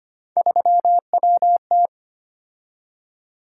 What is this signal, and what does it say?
Morse code, heard as a keyed tone, about 25 wpm, 700 Hz. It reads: 3WT
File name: Three Letter-Number Combinations - 25wpm